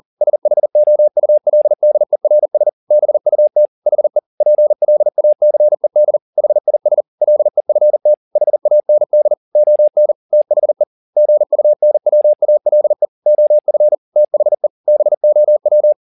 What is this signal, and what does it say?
Morse code, heard as a keyed tone, about 40 wpm, 600 Hz. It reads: SHOULDERS BUT HE PLACED HIS LEFT HAND ON THE GUNWALE OF THE BOW
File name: the old man and the sea 2 40wpm